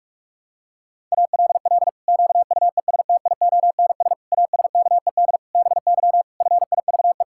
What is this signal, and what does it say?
Morse code, heard as a keyed tone, about 41 wpm, 700 Hz. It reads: ALL QUESTIONS ASKED BY FIVE
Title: Pangram Challenge v2 - 45wpm